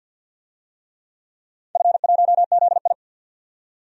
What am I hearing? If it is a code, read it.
Morse code, heard as a keyed tone, about 48 wpm, 700 Hz. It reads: F1ZI
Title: DXCC-100 Call Signs - 4 Characters - 50wpm